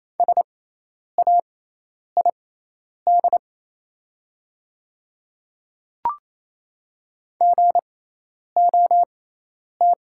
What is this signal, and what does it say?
Morse code, heard as a keyed tone, about 28 wpm, spaced as for 10 wpm, 700 Hz. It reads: SAID GOT
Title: 6x Character Spacing - Top 300 Words - 28wpm